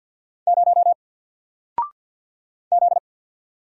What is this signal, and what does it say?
Morse code, heard as a keyed tone, about 50 wpm, 700 Hz. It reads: 0 Z
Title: Single Letter-Number 50wpm